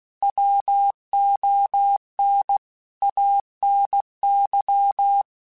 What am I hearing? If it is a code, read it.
Morse code, heard as a keyed tone, about 16 wpm, 800 Hz. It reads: WON ANY
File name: Quotes / Quotes-mp3s / Pessimism @15